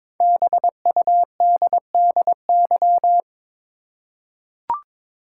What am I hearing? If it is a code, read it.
Morse code, heard as a keyed tone, about 22 wpm, 700 Hz. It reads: BUDDY
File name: 5 Letter Words - First 3 Letters Given - 22wpm